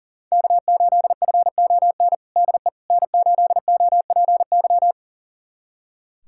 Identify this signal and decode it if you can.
Morse code, heard as a keyed tone, about 40 wpm, 700 Hz. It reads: K8FON DE N8OPY
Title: USA Callsign DE Callsign - 5 Characters - 40wpm